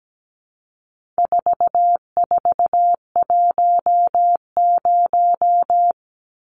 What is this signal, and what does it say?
Morse code, heard as a keyed tone, about 17 wpm, 700 Hz. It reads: 4410
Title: Numbers - 4-Digit 17wpm